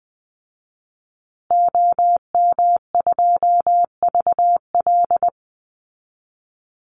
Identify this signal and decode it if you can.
Morse code, heard as a keyed tone, about 20 wpm, 700 Hz. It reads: OM2VL
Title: Call Signs Repeated 3x - 20wpm